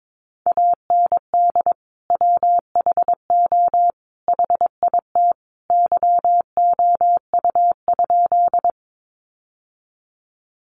Morse code, heard as a keyed tone, about 22 wpm, 700 Hz. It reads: AND WHO HIT YOU?
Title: Sentences from Top 1200 Words 22wpm